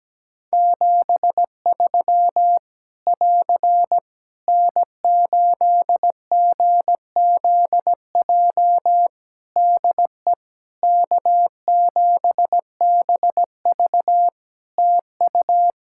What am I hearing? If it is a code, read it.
Morse code, heard as a keyed tone, about 17 wpm, 700 Hz. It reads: 73 <AR> N8GZJ DE K7BV TU
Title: Practice QSO - Exchange 4 - Ending - 17wpm